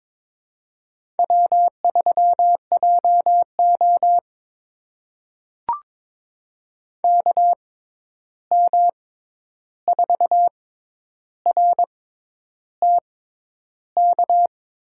Morse code, heard as a keyed tone, about 22 wpm, spaced as for 10 wpm, 700 Hz. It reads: W3JO KM4RTK
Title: USA Call Signs - 6x Character Spacing - 22wpm